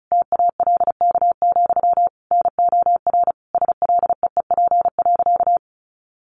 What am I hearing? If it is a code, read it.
Morse code, heard as a keyed tone, about 34 wpm, 700 Hz. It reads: TALK, NOR SLEEP.
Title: Sentences from Top 1500 Words 35wpm